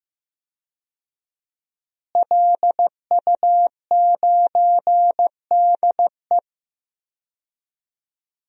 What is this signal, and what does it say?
Morse code, heard as a keyed tone, about 15 wpm, 700 Hz. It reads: LU9DE
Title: DXCC-100 Call Signs - 5 Characters - Repeated 3x - 15wpm